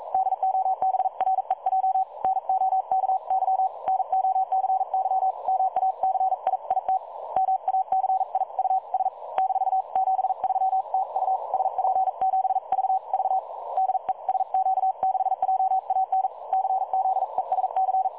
Morse code, heard as a keyed tone, about 42 wpm, 760 Hz. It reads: DOBREJ NOKY NOCY MACI IT MACIUS - 73 TU SP7VV DE SQ6JAN CZESC
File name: dlazaawansowanych / nagraniazpasma / websdr recording 3565.0kHz